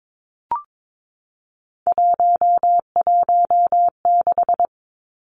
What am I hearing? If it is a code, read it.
Morse code, heard as a keyed tone, about 22 wpm, 700 Hz. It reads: 116
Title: Numbers - 3-Digit 22wpm